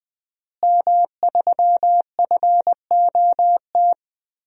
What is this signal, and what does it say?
Morse code, heard as a keyed tone, about 20 wpm, 700 Hz. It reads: M3FOT
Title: DXCC-100 Call Signs - 5 Characters - Repeated 3x - 20wpm